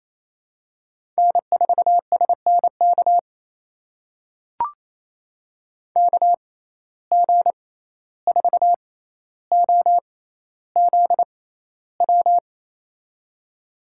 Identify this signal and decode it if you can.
Morse code, heard as a keyed tone, about 28 wpm, spaced as for 14 wpm, 700 Hz. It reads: N4SNK KG4OZW
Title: USA Call Signs - 6x Character Spacing - 28wpm